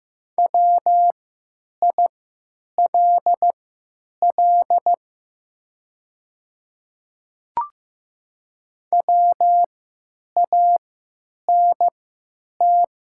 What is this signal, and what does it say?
Morse code, heard as a keyed tone, about 15 wpm, spaced as for 9 wpm, 700 Hz. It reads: WILL WANT
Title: 3x Character Spacing - Top 300 Words - 15wpm